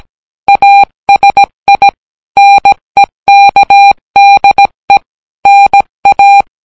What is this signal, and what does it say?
Morse code, heard as a keyed tone, about 17 wpm, 800 Hz. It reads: ASI NEKDE NA